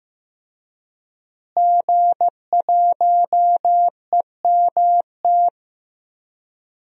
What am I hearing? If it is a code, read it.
Morse code, heard as a keyed tone, about 16 wpm, 700 Hz. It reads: G1EMT
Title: DXCC-100 Call Signs - 5 Characters - Repeated 2x - 15wpm